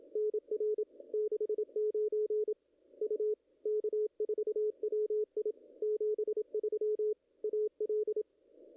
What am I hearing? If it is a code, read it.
Morse code, heard as a keyed tone, about 27 wpm, 430 Hz, with radio noise. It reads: NR 69 U K4WI 73 AL
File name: contests / ss-cw-k5tr-op-n5rz -0000